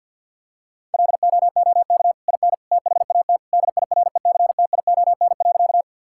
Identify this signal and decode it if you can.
Morse code, heard as a keyed tone, about 50 wpm, 700 Hz. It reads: LOOK IN THAT DIRECTION.
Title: Sentences from Top 1000 Words - Review Part 5 - 50wpm